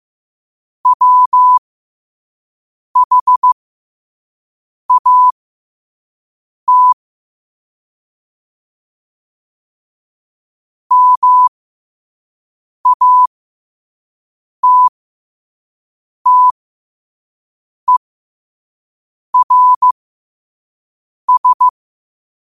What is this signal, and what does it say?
Morse code, heard as a keyed tone, about 15 wpm, spaced as for 5 wpm, 1000 Hz. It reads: WHAT MATTERS
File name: qotdfiles / 5wpm